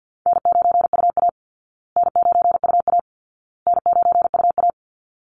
Morse code, heard as a keyed tone, about 50 wpm, 700 Hz. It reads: N9UA N9UA N9UA
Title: Easy Call Signs Repeated 3x - 50wpm